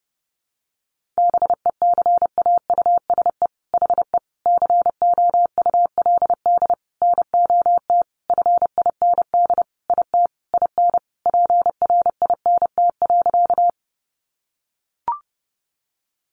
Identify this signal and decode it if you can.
Morse code, heard as a keyed tone, about 30 wpm, 700 Hz. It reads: BECAUSE HE COULD NOT FIND IT IN PRINT.
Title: Sentences from Top 1400 Words 30wpm